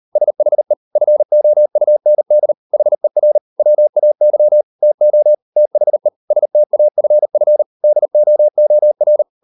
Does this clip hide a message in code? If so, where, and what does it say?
Yes: Morse code from 0.2 to 9.2 s, heard as a keyed tone, about 38 wpm, 600 Hz. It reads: SHE FOUND HER WAY TO THE STAFF DOOR